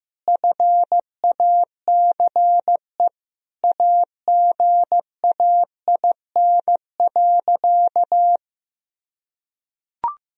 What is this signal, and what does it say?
Morse code, heard as a keyed tone, about 15 wpm, 700 Hz. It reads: FACE AGAIN.
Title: Sentences from Top 1000 Words - Review Part 2 - 15wpm